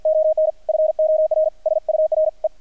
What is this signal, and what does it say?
Morse code, heard as a keyed tone, about 52 wpm, 630 Hz, with radio noise. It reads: ON FOR SURE
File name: archived / CRSnet